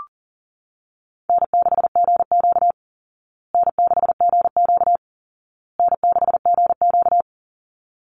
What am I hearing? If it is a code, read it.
Morse code, heard as a keyed tone, about 40 wpm, 700 Hz. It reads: N6GQ N6GQ N6GQ
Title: Easy Call Signs Repeated 3x - 40wpm